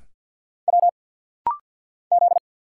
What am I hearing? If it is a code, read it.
Morse code, heard as a keyed tone, about 50 wpm, 700 Hz. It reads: W Z
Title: Single Letters - Rapid-Fire 50wpm